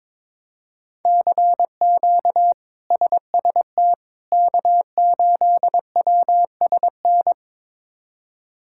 Morse code, heard as a keyed tone, about 22 wpm, 700 Hz. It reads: CQ SST K8WSN
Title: K1USN SST QSO Practice - 22wpm